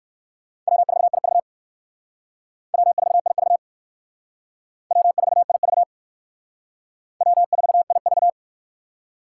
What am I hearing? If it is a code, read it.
Morse code, heard as a keyed tone, about 58 wpm, 700 Hz. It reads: W4IV W4IV W4IV W4IV